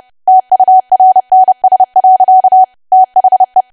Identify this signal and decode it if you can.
Morse code, heard as a keyed tone, about 30 wpm, 750 Hz. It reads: TURNS. THE
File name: cwa / W1AW-Level3 / 30WPM